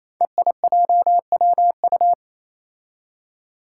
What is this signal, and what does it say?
Morse code, heard as a keyed tone, about 28 wpm, 700 Hz. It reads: EIJWU
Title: Five Challenging Letter-Number Combinations 28wpm